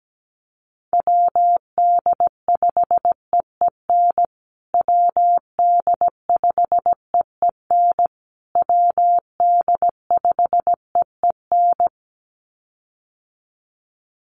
Morse code, heard as a keyed tone, about 17 wpm, 700 Hz. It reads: WD5EEN WD5EEN WD5EEN
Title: USA Call Signs Repeated 3x 17wpm